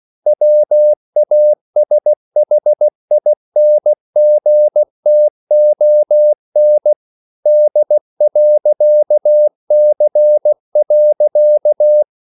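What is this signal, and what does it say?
Morse code, heard as a keyed tone, about 16 wpm, 600 Hz. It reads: WASHINGTON D.C.
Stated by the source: CW practice of national capitals at 16 words per minute, from ON6ZQ.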